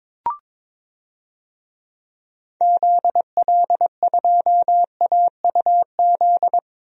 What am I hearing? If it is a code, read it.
Morse code, heard as a keyed tone, about 22 wpm, 700 Hz. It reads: ZL2AUZ
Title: DXCC-100 Call Signs - 6 Characters - Repeated 3x - 22wpm